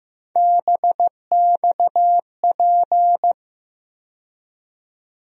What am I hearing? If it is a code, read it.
Morse code, heard as a keyed tone, about 15 wpm, 700 Hz. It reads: BXP
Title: USA Call Sign Postfixes - 3 Letters 15wpm